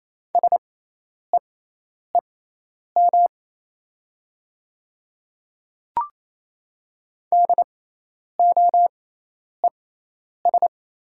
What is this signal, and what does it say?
Morse code, heard as a keyed tone, about 28 wpm, spaced as for 10 wpm, 700 Hz. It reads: SEEM DOES
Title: 6x Character Spacing - Top 300 Words - 28wpm